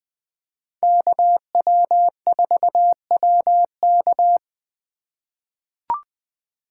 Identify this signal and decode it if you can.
Morse code, heard as a keyed tone, about 20 wpm, 700 Hz. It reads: KW4WK
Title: USA Call Signs - 4x Character Spacing - 20wpm